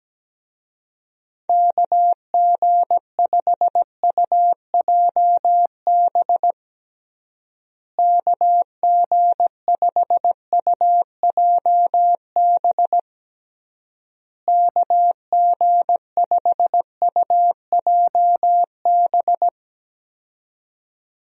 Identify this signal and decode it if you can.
Morse code, heard as a keyed tone, about 17 wpm, 700 Hz. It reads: KG5UJB KG5UJB KG5UJB
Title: USA Call Signs - 6 Characters - Repeated 3x - 17wpm